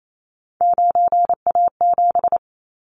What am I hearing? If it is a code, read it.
Morse code, heard as a keyed tone, about 28 wpm, 700 Hz. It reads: 9A7
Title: Callsign Prefixes for Top 100 DXCC - 28wpm